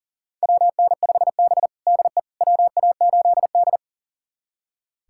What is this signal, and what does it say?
Morse code, heard as a keyed tone, about 37 wpm, 700 Hz. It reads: WN5B DE WA8D
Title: USA Callsign DE Callsign - 4 Characters - 40wpm